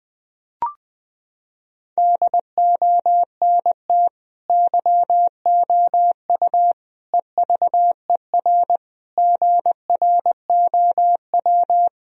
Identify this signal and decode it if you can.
Morse code, heard as a keyed tone, about 20 wpm, 700 Hz. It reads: DONT YOU EVER GROW
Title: Sentences from Top 1000 Words - Review Part 1 - 20wpm